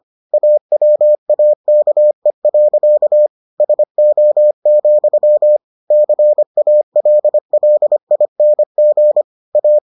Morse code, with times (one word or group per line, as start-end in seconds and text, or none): 0.3-3.3 AWAKE.
3.6-5.6 SO,
5.9-9.2 CALLING
9.6-9.8 A